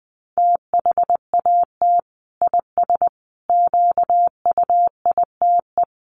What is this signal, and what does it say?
Morse code, heard as a keyed tone, about 20 wpm, 700 Hz. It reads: THAT IS QUITE
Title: Sentences from Top 1100 Words 20wpm